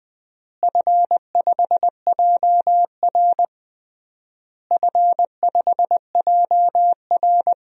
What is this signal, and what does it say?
Morse code, heard as a keyed tone, about 21 wpm, 700 Hz. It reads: F5JR F5JR
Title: DXCC-100 Call Signs - 4 Characters - Repeated 3x - 20wpm